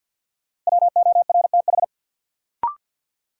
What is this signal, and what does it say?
Morse code, heard as a keyed tone, about 51 wpm, 700 Hz. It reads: WORTH
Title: Sentences from Top 800 Words - New Word Drill - 50wpm